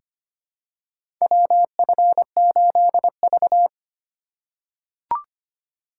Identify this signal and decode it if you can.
Morse code, heard as a keyed tone, about 25 wpm, 700 Hz. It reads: WF8V
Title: USA Call Signs - 4 Characters - 6x Character Spacing - 25wpm